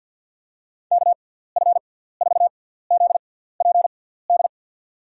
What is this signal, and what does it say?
Morse code, heard as a keyed tone, about 53 wpm, spaced as for 17 wpm, 700 Hz. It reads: KF4ZPD
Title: USA Call Signs - 6x Character Spacing - 50wpm